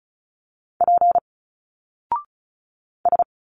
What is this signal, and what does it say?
Morse code, heard as a keyed tone, about 35 wpm, 700 Hz. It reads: P S
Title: Single Letters 35wpm